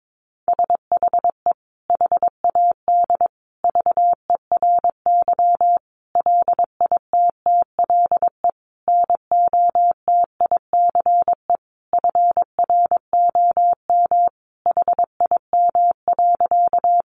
Morse code, heard as a keyed tone, about 22 wpm, 700 Hz. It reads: SHE HAD VERY LITTLE NOTICE FROM HIM.
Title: Sentences from Top 400 Words 22wpm